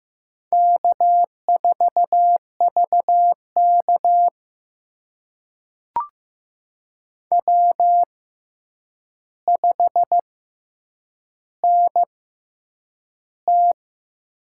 Morse code, heard as a keyed tone, about 15 wpm, spaced as for 8 wpm, 700 Hz. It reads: K4VK W5NT
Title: USA Call Signs - 4 Characters - 6x Character Spacing - 15wpm